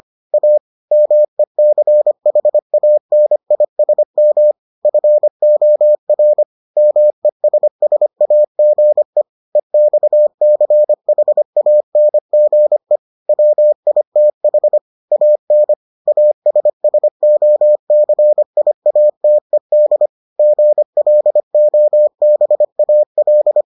Morse code, heard as a keyed tone, about 25 wpm, 600 Hz. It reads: A MECHANISM FOR MESSAGE EXCHANGE WITH AN ASSOCIATED GLOBAL